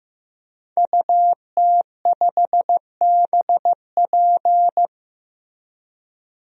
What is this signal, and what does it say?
Morse code, heard as a keyed tone, about 15 wpm, 700 Hz. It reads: UT5BP
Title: DXCC-100 Call Signs - 5 Characters - 15wpm